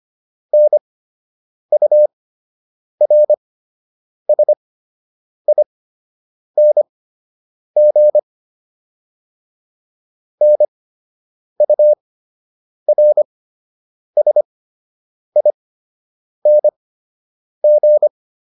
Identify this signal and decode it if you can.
Morse code, heard as a keyed tone, about 25 wpm, spaced as for 8 wpm, 600 Hz. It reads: NURSING NURSING